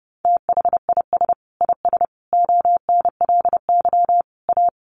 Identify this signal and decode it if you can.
Morse code, heard as a keyed tone, about 30 wpm, 700 Hz. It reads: THIS IS ONLY A
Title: Sentences from Top 1400 Words 30wpm